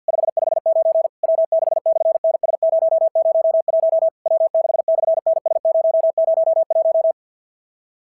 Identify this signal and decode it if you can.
Morse code, heard as a keyed tone, about 53 wpm, 650 Hz. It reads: 559 W6/NS001 W6/NS001
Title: SOTA QSO Practice 50wpm